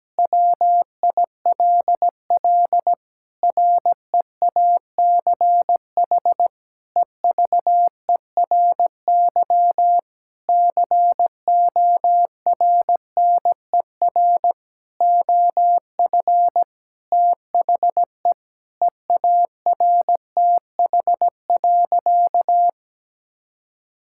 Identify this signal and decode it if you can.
Morse code, heard as a keyed tone, about 17 wpm, 700 Hz. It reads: WILL REACH EVERY CORNER OF THE EARTH.
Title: Sentences from Top 1600 Words 17wpm